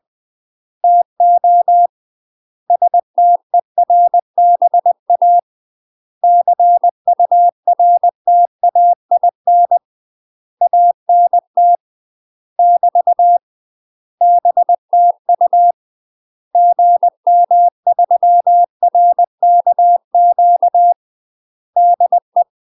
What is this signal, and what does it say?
Morse code, heard as a keyed tone, about 20 wpm, 700 Hz. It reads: TO STERBA CURTAIN ANT <BT> BTU GM3RKQ DE
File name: UK MASTER QSOs TRAINING / 20WPM / UK MASTER QSOs 0029